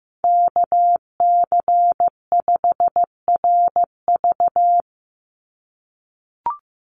Morse code, heard as a keyed tone, about 15 wpm, 700 Hz. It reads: KC5RV
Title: USA Call Signs 15wpm